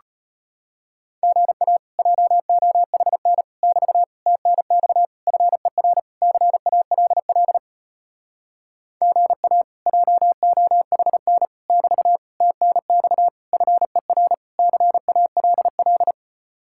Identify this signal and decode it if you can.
Morse code, heard as a keyed tone, about 33 wpm, 700 Hz. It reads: GA JOHN <BT> TNX FER CALL GA JOHN <BT> TNX FER CALL